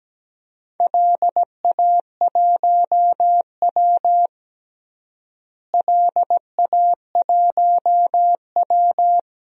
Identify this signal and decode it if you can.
Morse code, heard as a keyed tone, about 17 wpm, 700 Hz. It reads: LA1W LA1W
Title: DXCC-100 Call Signs - 4 Characters - Repeated 3x - 17wpm